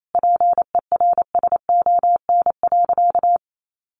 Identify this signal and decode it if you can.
Morse code, heard as a keyed tone, about 28 wpm, 700 Hz. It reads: PERSON.
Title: Sentences from Top 1400 Words 28wpm